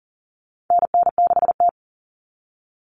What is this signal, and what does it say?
Morse code, heard as a keyed tone, about 40 wpm, 700 Hz. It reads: NN6T
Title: Call Signs - 40wpm